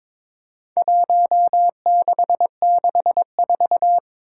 Morse code, heard as a keyed tone, about 22 wpm, 700 Hz. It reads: 1664
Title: Four Challenging Letter-Number Combinations 22wpm